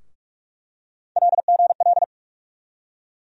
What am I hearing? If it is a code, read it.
Morse code, heard as a keyed tone, about 42 wpm, 700 Hz. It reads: LGL
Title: USA Call Sign Postfixes - 3 Letters 45wpm